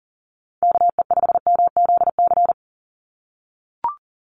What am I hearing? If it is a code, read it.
Morse code, heard as a keyed tone, about 40 wpm, 690 Hz. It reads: KE5MZC
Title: USA Call Signs 40wpm